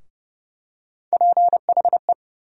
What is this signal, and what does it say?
Morse code, heard as a keyed tone, about 32 wpm, 700 Hz. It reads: PHE
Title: USA Call Sign Postfixes - 3 Letters 30wpm